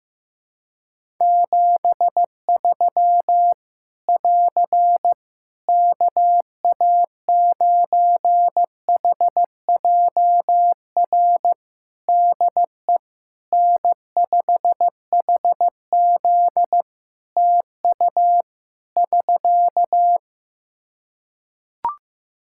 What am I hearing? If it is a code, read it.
Morse code, heard as a keyed tone, about 15 wpm, 700 Hz. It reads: 73 <AR> KA9HJR DE N5HZ TU <SK>
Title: Practice QSO - Exchange 4 - Ending - 15wpm